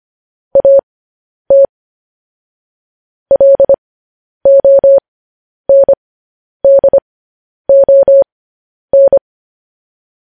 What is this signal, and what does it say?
Morse code, heard as a keyed tone, about 25 wpm, spaced as for 10 wpm, 550 Hz. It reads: AT LONDON